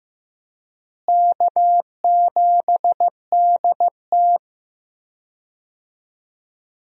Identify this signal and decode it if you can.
Morse code, heard as a keyed tone, about 15 wpm, 700 Hz. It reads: K7DT
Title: USA Call Signs - 4 Characters - Repeated 2x - 15wpm